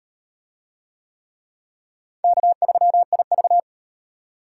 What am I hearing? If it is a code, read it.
Morse code, heard as a keyed tone, about 33 wpm, 700 Hz. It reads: K3IV